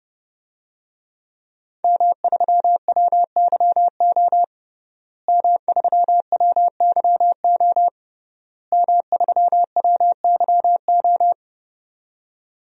Morse code, heard as a keyed tone, about 30 wpm, 700 Hz. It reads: M3WYO M3WYO M3WYO
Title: DXCC-100 Call Signs - 5 Characters - Repeated 3x - 30wpm